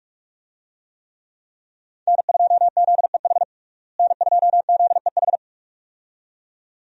Morse code, heard as a keyed tone, about 46 wpm, 700 Hz. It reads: N2ZEH N2ZEH
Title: USA Call Signs - 5 Characters - Repeated 2x - 45wpm